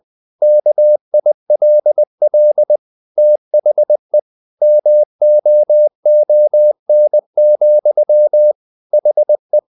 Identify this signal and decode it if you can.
Morse code, heard as a keyed tone, about 20 wpm, 600 Hz. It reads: KILL THE MOON, HE
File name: the old man and the sea part 2 20wpm